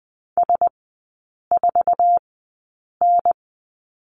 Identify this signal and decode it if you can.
Morse code, heard as a keyed tone, about 20 wpm, spaced as for 8 wpm, 700 Hz. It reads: S4N
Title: 047 Lesson 9e - Number 4 - Three Characters 20wpm